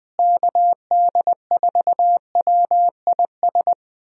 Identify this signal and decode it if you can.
Morse code, heard as a keyed tone, about 20 wpm, 700 Hz. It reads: KD4WIS
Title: USA Call Signs - 6 Characters - Repeated 2x - 20wpm